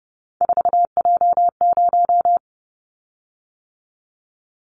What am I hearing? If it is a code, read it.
Morse code, heard as a keyed tone, about 32 wpm, 700 Hz. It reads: 4J0
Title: Callsign Prefixes for Top 100 DXCC - 30wpm